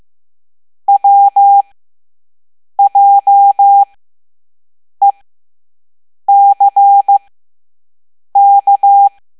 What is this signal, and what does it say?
Morse code, heard as a keyed tone, about 15 wpm, spaced as for 6 wpm, 800 Hz. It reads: WJECK
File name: Morse / Level7-All-A